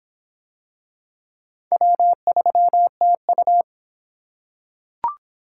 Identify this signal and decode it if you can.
Morse code, heard as a keyed tone, about 23 wpm, 700 Hz. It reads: W3TU